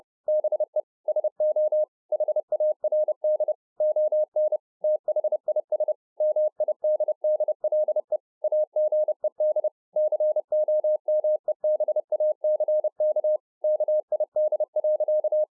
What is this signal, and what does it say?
Morse code, heard as a keyed tone, about 30 wpm, 610 Hz. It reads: BE SO HARD ON THIS MIDDLE AGED COMEBACK KID.